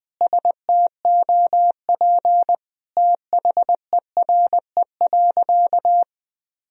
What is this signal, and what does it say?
Morse code, heard as a keyed tone, about 19 wpm, 700 Hz. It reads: STOP THERE.
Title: Sentences from Top 1000 Words - Review Part 1 - 20wpm